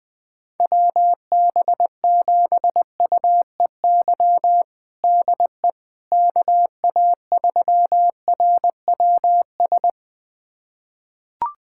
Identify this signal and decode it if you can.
Morse code, heard as a keyed tone, about 20 wpm, 700 Hz. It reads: WB7UEY DE KA3RWS
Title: USA Callsign DE Callsign - 6 Characters - 20wpm